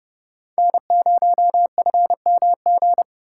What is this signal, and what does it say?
Morse code, heard as a keyed tone, about 28 wpm, 700 Hz. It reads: N0FMG